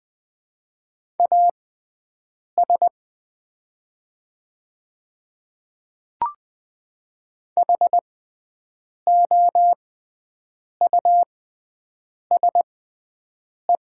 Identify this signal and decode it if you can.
Morse code, heard as a keyed tone, about 20 wpm, spaced as for 7 wpm, 700 Hz. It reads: AS HOUSE
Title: 6x Character Spacing - Top 300 Words - 20wpm